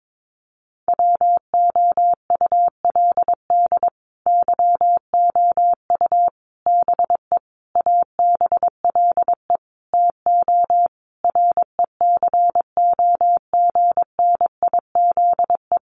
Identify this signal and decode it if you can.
Morse code, heard as a keyed tone, about 22 wpm, 700 Hz. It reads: WOULD YOU BE ABLE TO RECOGNIZE
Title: Sentences from Top 1400 Words 22wpm